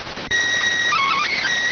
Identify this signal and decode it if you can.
SSTV (slow-scan television), PD 290.